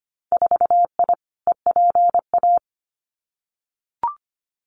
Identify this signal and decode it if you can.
Morse code, heard as a keyed tone, about 25 wpm, 700 Hz. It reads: 4I EPA
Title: Winter Field Day Exchanges Encore 25wpm